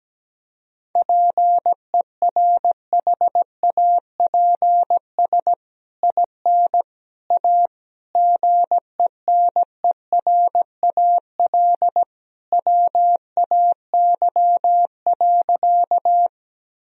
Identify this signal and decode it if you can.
Morse code, heard as a keyed tone, about 17 wpm, 700 Hz. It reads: PERHAPS IN A GENERAL WAY.
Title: Sentences from Top 1000 Words - Review Part 3 - 17wpm